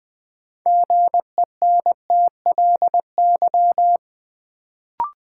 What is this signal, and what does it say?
Morse code, heard as a keyed tone, about 20 wpm, 700 Hz. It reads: GENTLY
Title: Sentences from Top 1500 Words - New Word Drill Alt - 20wpm